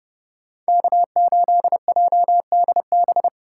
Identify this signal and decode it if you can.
Morse code, heard as a keyed tone, about 30 wpm, 700 Hz. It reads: K8JDB